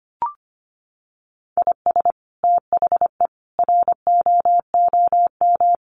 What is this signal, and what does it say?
Morse code, heard as a keyed tone, about 25 wpm, 700 Hz. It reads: IS THE ROOM
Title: Sentences from Top 1400 Words 25wpm